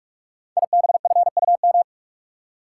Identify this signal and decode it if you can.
Morse code, heard as a keyed tone, about 44 wpm, 700 Hz. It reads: IBFUM
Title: Code Groups - 5 Letter:Numbers - 45wpm